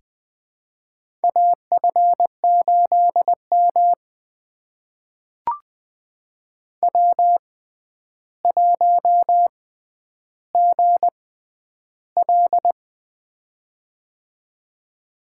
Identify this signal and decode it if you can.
Morse code, heard as a keyed tone, about 20 wpm, spaced as for 10 wpm, 700 Hz. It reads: AF8M W1GL
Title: USA Call Signs - 4 Characters - 6x Character Spacing - 20wpm